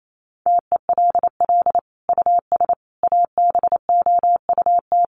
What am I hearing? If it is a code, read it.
Morse code, heard as a keyed tone, about 28 wpm, 700 Hz. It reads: TELL US ABOUT
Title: Sentences from Top 1200 Words 28wpm